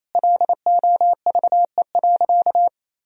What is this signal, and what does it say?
Morse code, heard as a keyed tone, about 28 wpm, 700 Hz. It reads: LOVE.
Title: Sentences from Top 1000 Words - Review Part 3 - 28wpm